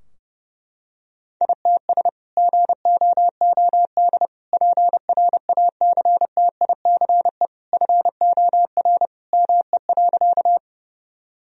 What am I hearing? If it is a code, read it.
Morse code, heard as a keyed tone, about 30 wpm, 700 Hz. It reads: ITS GOOD PRACTICE FOR ME.
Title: Sentences from Top 1000 Words - Review Part 4 - 30wpm